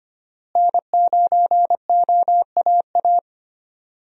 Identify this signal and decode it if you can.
Morse code, heard as a keyed tone, about 25 wpm, 700 Hz. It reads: N9OAA
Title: USA Call Signs - 5 Characters - Repeated 2x - 25wpm